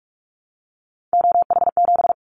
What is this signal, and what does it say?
Morse code, heard as a keyed tone, about 46 wpm, 700 Hz. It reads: OH7
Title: Callsign Prefixes for Top 50 DXCC - 45wpm